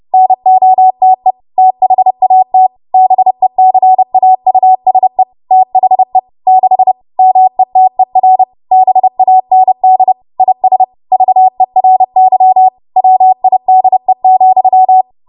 Morse code, heard as a keyed tone, about 30 wpm, 750 Hz. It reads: NOTE THAT BECAUSE THE 6 METER BAND IS VERY WIDE,
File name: Morse / 30WPM